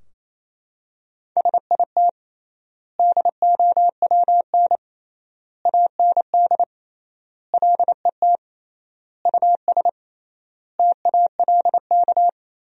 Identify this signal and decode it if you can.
Morse code, heard as a keyed tone, about 27 wpm, 700 Hz. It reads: SIT DOWN AND LET US TALK
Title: 3x Word-Spacing - Sentences from Top 500 Words - 28wpm